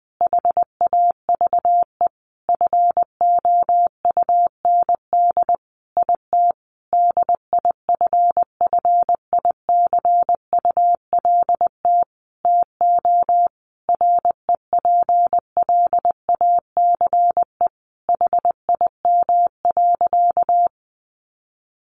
Morse code, heard as a keyed tone, about 20 wpm, 700 Hz. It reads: HAVE FOUND IT DIFFICULT TO REPLACE HIM.
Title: Sentences from Top 1400 Words 20wpm